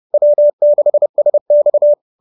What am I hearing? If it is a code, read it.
Morse code, heard as a keyed tone, about 30 wpm, 570 Hz. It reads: W6SX